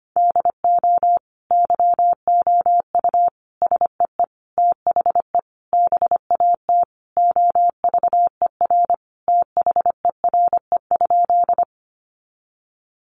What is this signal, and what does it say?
Morse code, heard as a keyed tone, about 25 wpm, 700 Hz. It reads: DO YOU SEE THE BAT OVER THERE?
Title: Sentences from Top 1500 Words 25wpm